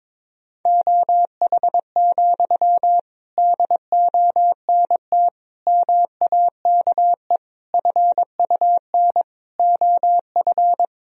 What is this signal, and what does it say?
Morse code, heard as a keyed tone, about 21 wpm, 700 Hz. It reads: OH, DONT MAKE FUN OF
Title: Sentences from Top 1000 Words - Review Part 4 - 22wpm